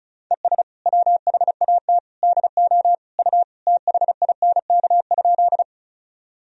Morse code, heard as a keyed tone, about 36 wpm, 700 Hz. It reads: ES WHAT DO U THINK?
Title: Sentences from Top 200 Words and Abbreviations 35wpm